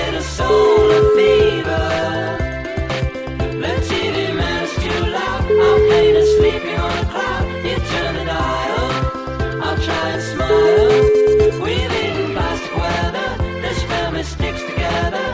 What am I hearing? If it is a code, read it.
Ringback tone (Europe, 425 Hz), with noise.